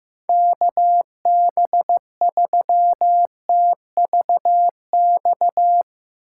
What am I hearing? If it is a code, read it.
Morse code, heard as a keyed tone, about 15 wpm, 700 Hz. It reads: KB3TVX
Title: USA Call Signs - 6 Characters - 2x Character Spacing - 15wpm